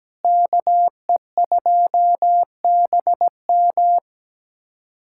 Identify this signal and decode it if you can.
Morse code, heard as a keyed tone, about 17 wpm, 700 Hz. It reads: KE2BM
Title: USA Callsign DE Callsign - 5 Characters - 17wpm